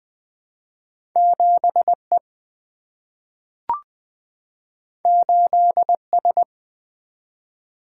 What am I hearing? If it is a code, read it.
Morse code, heard as a keyed tone, about 20 wpm, 700 Hz. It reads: 7E 8S
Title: Single-Letter Single-Number 20wpm